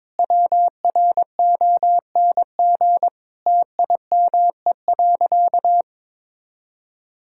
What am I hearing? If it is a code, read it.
Morse code, heard as a keyed tone, about 22 wpm, 700 Hz. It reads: WRONG TIME.
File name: Sentences with 7-Word Phrases - 22wpm